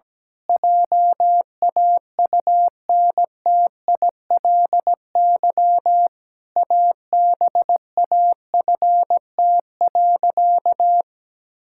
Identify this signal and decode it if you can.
Morse code, heard as a keyed tone, about 17 wpm, 700 Hz. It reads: JAUNTILY ABAFT.
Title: Pangram Challenge v2 - 17wpm